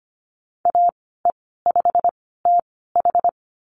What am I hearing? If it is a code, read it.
Morse code, heard as a keyed tone, about 25 wpm, spaced as for 16 wpm, 690 Hz. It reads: AE5TH
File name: Lesson 15g - Number 5 - Call Signs 25wpm